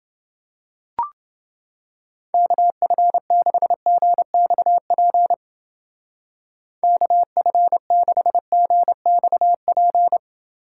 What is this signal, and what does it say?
Morse code, heard as a keyed tone, about 29 wpm, 700 Hz. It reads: KF6GXP KF6GXP